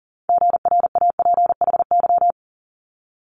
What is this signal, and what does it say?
Morse code, heard as a keyed tone, about 40 wpm, 700 Hz. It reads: GRAPHY
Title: Multi-letter Phonemes Quick Fire 40wpm